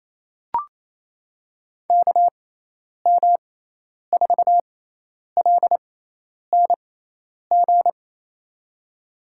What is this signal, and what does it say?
Morse code, heard as a keyed tone, about 27 wpm, spaced as for 10 wpm, 700 Hz. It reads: KM4LNG
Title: USA Call Signs - 6x Character Spacing - 28wpm